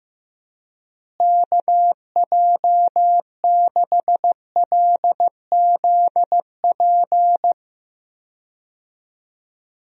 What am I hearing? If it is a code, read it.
Morse code, heard as a keyed tone, about 15 wpm, 700 Hz. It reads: KJ6LZP
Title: USA Call Signs - 6 Characters - Repeated 2x - 15wpm